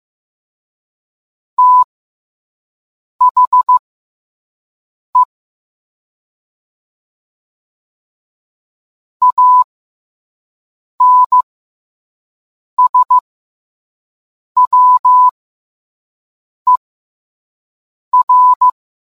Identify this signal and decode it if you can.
Morse code, heard as a keyed tone, about 15 wpm, spaced as for 5 wpm, 1000 Hz. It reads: THE ANSWER